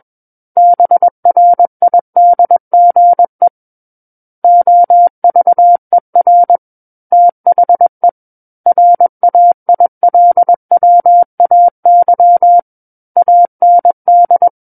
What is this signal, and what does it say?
Morse code, heard as a keyed tone, about 21 wpm, 700 Hz. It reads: BRIDGE OVER THE RAILWAY AND
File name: Book One / 12-Chapter 12 - 21 WPM